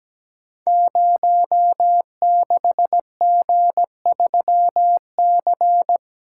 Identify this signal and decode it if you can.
Morse code, heard as a keyed tone, about 17 wpm, 700 Hz. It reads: 06G3C
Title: Code Groups - 5 Letter:Numbers - 17wpm